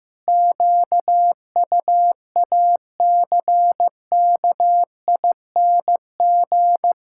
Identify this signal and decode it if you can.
Morse code, heard as a keyed tone, about 15 wpm, 700 Hz. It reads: QUACKING
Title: Pangram Challenge v2 - 15wpm